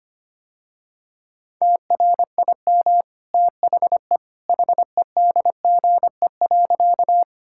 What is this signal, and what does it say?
Morse code, heard as a keyed tone, about 25 wpm, 700 Hz. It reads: TRIM THE HEDGE.
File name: Easy Sentences with Hints - 25wpm